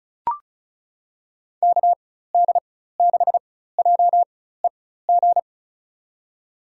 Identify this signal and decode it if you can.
Morse code, heard as a keyed tone, about 36 wpm, spaced as for 16 wpm, 700 Hz. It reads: KD6JEG
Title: USA Call Signs - 6 Characters - 4x Character Spacing - 35wpm